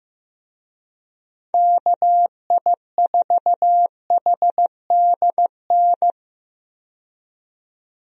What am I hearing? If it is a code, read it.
Morse code, heard as a keyed tone, about 15 wpm, 700 Hz. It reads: KI4HDN
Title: USA Call Signs - 6 Characters - 15wpm